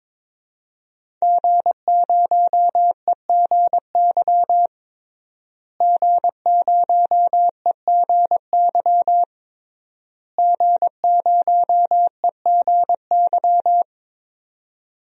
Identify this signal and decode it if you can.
Morse code, heard as a keyed tone, about 22 wpm, 700 Hz. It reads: G0EGY G0EGY G0EGY
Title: DXCC-100 Call Signs - 5 Characters - Repeated 3x - 22wpm